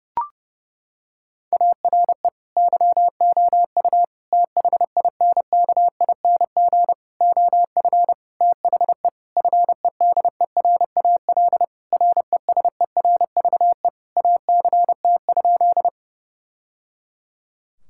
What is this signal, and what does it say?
Morse code, heard as a keyed tone, about 29 wpm, 700 Hz. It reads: ARE YOU THINKING OF THE FEDERAL RESERVE ACT?
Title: Sentences from Top 1000 Words - Review Part 6 - 30wpm